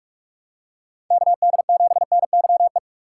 Morse code, heard as a keyed tone, about 42 wpm, 700 Hz. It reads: KD7NYE
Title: USA Call Signs - 6 Characters - 2x Character Spacing - 45wpm